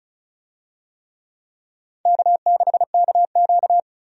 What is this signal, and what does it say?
Morse code, heard as a keyed tone, about 36 wpm, 700 Hz. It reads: K6KQ
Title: USA Call Signs - 4 Characters - 35wpm